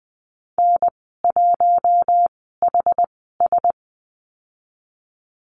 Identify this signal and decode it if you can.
Morse code, heard as a keyed tone, about 20 wpm, spaced as for 14 wpm, 700 Hz. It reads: N1HS
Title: 069 Lesson 12g - Letter d - Call Signs 20wpm